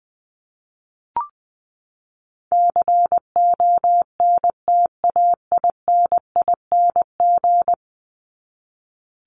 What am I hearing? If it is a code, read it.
Morse code, heard as a keyed tone, about 20 wpm, 700 Hz. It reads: CONTAINING
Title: ING Suffix - Top 100 Words with Base Words 20wpm